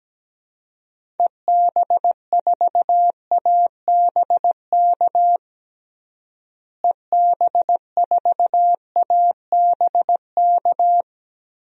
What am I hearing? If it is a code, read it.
Morse code, heard as a keyed tone, about 17 wpm, 700 Hz. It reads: EB4ABK EB4ABK
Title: DXCC-100 Call Signs - 6 Characters - Repeated 3x - 17wpm